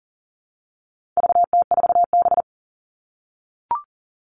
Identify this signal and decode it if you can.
Morse code, heard as a keyed tone, about 40 wpm, 700 Hz. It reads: VT4B
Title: All Cut Numbers - 4-Digit 40wpm